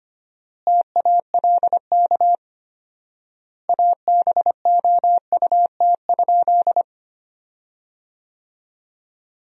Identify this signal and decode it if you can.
Morse code, heard as a keyed tone, about 25 wpm, 700 Hz. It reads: TALK ABOUT?
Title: 4x Word-Spacing - Sentences from Top 400 Words - 25wpm